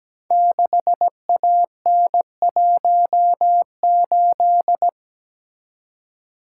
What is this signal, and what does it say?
Morse code, heard as a keyed tone, about 17 wpm, 700 Hz. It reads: 6AN18
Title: Code Groups - 5 Letter:Numbers - 17wpm